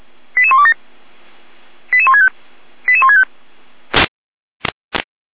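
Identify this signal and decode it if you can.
Five-tone selective call (ZVEI), heard as quick runs of five tones, tones of about 70 ms. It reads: 88127 88165 88165
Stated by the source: ZVEI five-tone call received by SDR.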